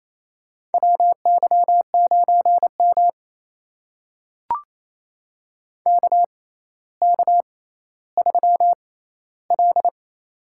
Morse code, heard as a keyed tone, about 28 wpm, spaced as for 14 wpm, 700 Hz. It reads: WY9M KK3L
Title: USA Call Signs - 4 Characters - 6x Character Spacing - 28wpm